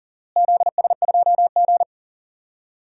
Morse code, heard as a keyed tone, about 40 wpm, 700 Hz. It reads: ZS2G
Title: DXCC-100 Call Signs - 4 Characters - Repeated 2x - 40wpm